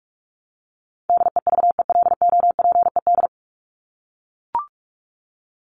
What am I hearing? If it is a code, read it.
Morse code, heard as a keyed tone, about 44 wpm, 700 Hz. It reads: DEVELOPED
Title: ED Suffix - Top 100 Words with Base Words 45wpm